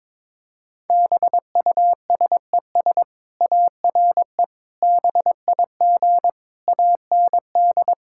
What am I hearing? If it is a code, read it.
Morse code, heard as a keyed tone, about 22 wpm, 700 Hz. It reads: BUSES ARE BIG AND
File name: practice / 22wpm / Easy Sentences with Hints - 22wpm